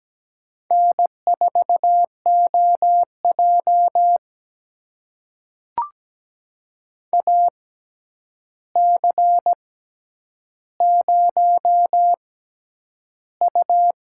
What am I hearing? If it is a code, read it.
Morse code, heard as a keyed tone, about 17 wpm, spaced as for 9 wpm, 700 Hz. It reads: N4OJ AC0U
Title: USA Call Signs - 4 Characters - 6x Character Spacing - 17wpm